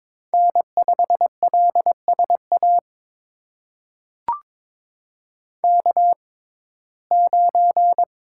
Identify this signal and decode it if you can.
Morse code, heard as a keyed tone, about 22 wpm, spaced as for 16 wpm, 700 Hz. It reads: N5LSA K9
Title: USA Call Signs - 5 Characters - 6x Character Spacing - 22wpm